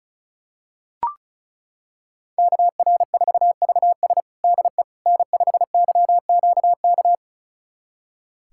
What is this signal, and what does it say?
Morse code, heard as a keyed tone, about 36 wpm, 700 Hz. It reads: KR4VS DE N5YQK
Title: USA Callsign DE Callsign - 5 Characters - 35wpm